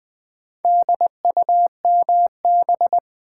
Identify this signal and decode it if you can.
Morse code, heard as a keyed tone, about 20 wpm, 700 Hz. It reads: DUMB
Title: Build A Word - 2 Letters At A Time - 4-Letter Words - 20wpm